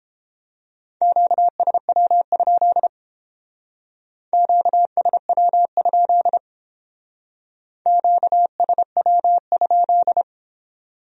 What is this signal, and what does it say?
Morse code, heard as a keyed tone, about 30 wpm, 700 Hz. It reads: QSW? QSW? QSW?